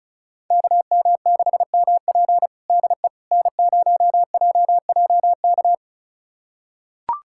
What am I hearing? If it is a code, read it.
Morse code, heard as a keyed tone, about 34 wpm, 700 Hz. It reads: KM6MP DE N0JJK
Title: USA Callsign DE Callsign - 5 Characters - 35wpm